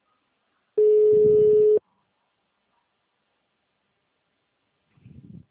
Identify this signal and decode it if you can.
Ringback tone (Europe, 425 Hz).